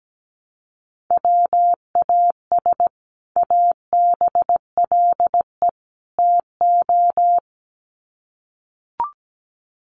Morse code, heard as a keyed tone, about 17 wpm, 700 Hz. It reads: WAS ABLE TO
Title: Sets of 3 Words 17wpm